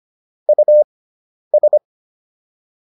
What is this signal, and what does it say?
Morse code, heard as a keyed tone, about 25 wpm, spaced as for 10 wpm, 600 Hz. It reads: US